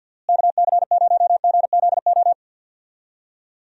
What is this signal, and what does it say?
Morse code, heard as a keyed tone, about 51 wpm, 700 Hz. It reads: KC0GZO
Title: USA Call Signs - 6 Characters - Repeated 2x - 50wpm